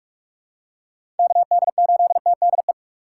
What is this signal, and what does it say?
Morse code, heard as a keyed tone, about 42 wpm, 700 Hz. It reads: KD8TDE
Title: USA Call Signs - 2x Character Spacing - 45wpm